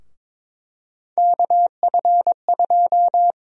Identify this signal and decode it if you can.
Morse code, heard as a keyed tone, about 22 wpm, 700 Hz. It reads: KF2
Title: USA Call Sign Prefixes - Repeated 2x - 22wpm